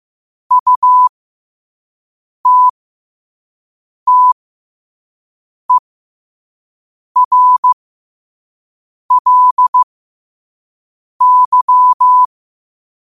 Morse code, heard as a keyed tone, about 15 wpm, spaced as for 5 wpm, 1000 Hz. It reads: UTTERLY
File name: qotdfiles / 5wpm